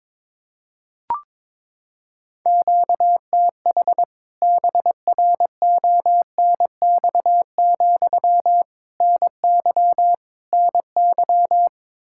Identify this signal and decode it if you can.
Morse code, heard as a keyed tone, about 22 wpm, 700 Hz. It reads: QTH BRONX, NY NY
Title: K1USN SST QSO Practice 22wpm